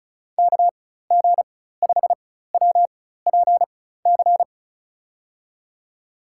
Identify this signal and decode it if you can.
Morse code, heard as a keyed tone, about 36 wpm, spaced as for 16 wpm, 700 Hz. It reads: KG5WPC
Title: USA Call Signs - 4x Character Spacing - 35wpm